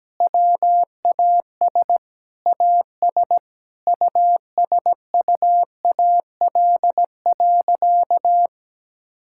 Morse code, heard as a keyed tone, about 17 wpm, 700 Hz. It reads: WAS AS USUAL.
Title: Sentences from Top 1000 Words - Review Part 5 - 17wpm